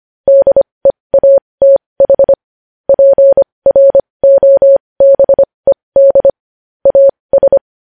Morse code, heard as a keyed tone, about 25 wpm, spaced as for 20 wpm, 550 Hz. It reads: DEATH PROBED AS
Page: News Headlines in Morse Code at 20 WPM